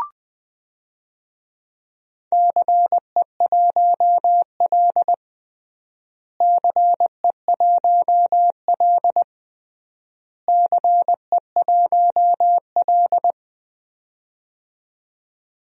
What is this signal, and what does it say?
Morse code, heard as a keyed tone, about 21 wpm, 700 Hz. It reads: CE1L CE1L CE1L
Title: DXCC-100 Call Signs - 4 Characters - Repeated 3x - 20wpm